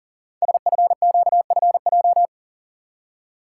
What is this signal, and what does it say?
Morse code, heard as a keyed tone, about 38 wpm, 700 Hz. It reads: SFQFJ
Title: Code Groups - 5 Letters - 40wpm